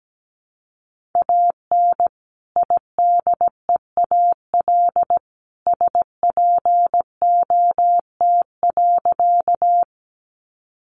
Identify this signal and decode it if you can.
Morse code, heard as a keyed tone, about 17 wpm, 700 Hz. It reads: AN IDEAL SPOT.
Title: Sentences from Top 1400 Words 17wpm